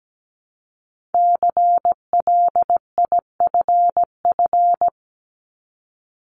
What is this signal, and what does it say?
Morse code, heard as a keyed tone, about 17 wpm, 700 Hz. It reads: CLIFF
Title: Multi-letter Phonemes 17wpm